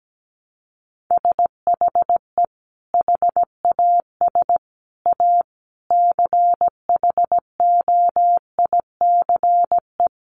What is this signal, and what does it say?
Morse code, heard as a keyed tone, about 17 wpm, 700 Hz. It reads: SHE HAS A CHOICE
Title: Sentences from Top 900 Words 17wpm